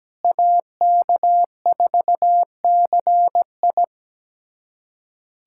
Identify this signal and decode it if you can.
Morse code, heard as a keyed tone, about 17 wpm, 700 Hz. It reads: AK4CI
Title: USA Callsign DE Callsign - 5 Characters - 17wpm